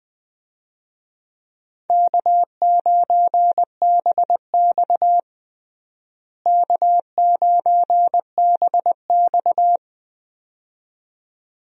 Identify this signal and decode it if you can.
Morse code, heard as a keyed tone, about 21 wpm, 700 Hz. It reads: K9BX K9BX
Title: USA Call Signs - 4 Characters - Repeated 2x - 20wpm